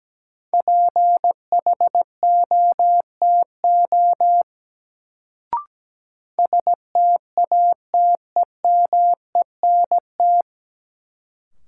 Morse code, heard as a keyed tone, about 17 wpm, 700 Hz. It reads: PHOTO STATEMENT
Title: Sentences from Top 1200 Words - New Word Drill Alt - 17wpm